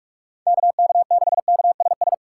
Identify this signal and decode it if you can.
Morse code, heard as a keyed tone, about 46 wpm, 690 Hz. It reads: KK6KSS
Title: USA Call Signs - 6 Characters - 2x Character Spacing - 45wpm